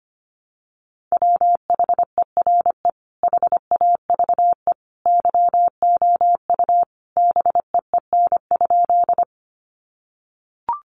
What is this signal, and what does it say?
Morse code, heard as a keyed tone, about 25 wpm, 700 Hz. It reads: WHERE HAVE YOU BEEN?
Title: Sentences from Top 200 Words 25wpm